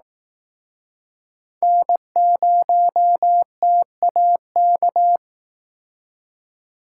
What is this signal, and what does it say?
Morse code, heard as a keyed tone, about 19 wpm, 700 Hz. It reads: N0TAK
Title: Speed Racing - USA 1x3 Call Signs 15wpm